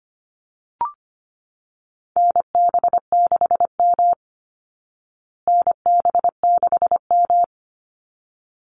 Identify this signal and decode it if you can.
Morse code, heard as a keyed tone, about 25 wpm, 700 Hz. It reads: NB6M NB6M
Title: Call Signs Repeated 3x - 25wpm